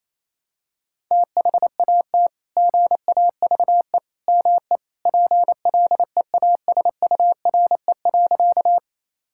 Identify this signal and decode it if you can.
Morse code, heard as a keyed tone, about 28 wpm, 700 Hz. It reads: THAT GAVE ME PLEASURE.
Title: Sentences from Top 1600 Words 28wpm